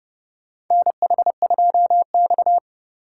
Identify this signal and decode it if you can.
Morse code, heard as a keyed tone, about 32 wpm, 700 Hz. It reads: NH2X
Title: USA Call Signs - 4 Characters - 4x Character Spacing - 30wpm